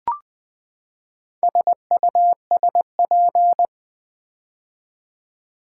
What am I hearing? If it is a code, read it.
Morse code, heard as a keyed tone, about 20 wpm, 700 Hz. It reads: SUSP
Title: Build A Word - 4 Letters At A Time - 10-Letter Words - 20wpm